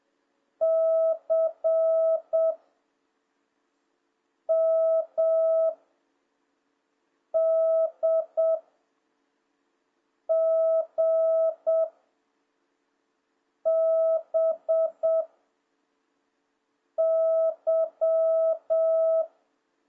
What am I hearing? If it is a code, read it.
Morse code, heard as a keyed tone, about 7 wpm, 660 Hz. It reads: C M D G B Y